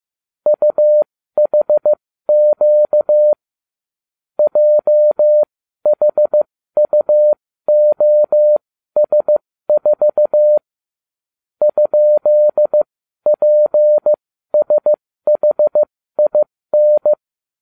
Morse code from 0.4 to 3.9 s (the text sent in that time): UHQ